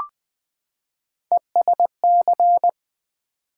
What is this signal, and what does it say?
Morse code, heard as a keyed tone, about 20 wpm, 700 Hz. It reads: ESC
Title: Build A Word - 3 Letters At A Time - 6-Letter Words - 20wpm